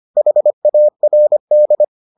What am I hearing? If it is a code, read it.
Morse code, heard as a keyed tone, about 25 wpm, 600 Hz. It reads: HARD